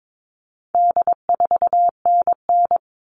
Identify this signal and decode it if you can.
Morse code, heard as a keyed tone, about 22 wpm, 700 Hz. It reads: D4NN
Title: All Cut Numbers - 4-Digit 22wpm